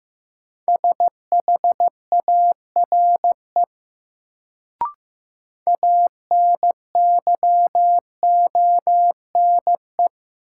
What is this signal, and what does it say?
Morse code, heard as a keyed tone, about 15 wpm, 700 Hz. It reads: SHARE ANYONE
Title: Sentences from Top 900 Words - New Word Drill - 15wpm